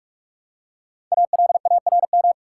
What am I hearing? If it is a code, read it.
Morse code, heard as a keyed tone, about 46 wpm, 700 Hz. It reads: ALARM
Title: Five Letter Words - 45wpm